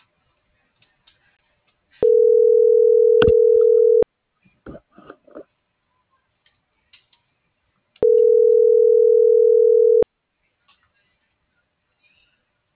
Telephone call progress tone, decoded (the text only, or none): ringback tone